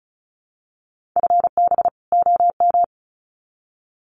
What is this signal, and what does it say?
Morse code, heard as a keyed tone, about 34 wpm, 700 Hz. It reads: FB OM
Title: Common QSO Phrases 35wpm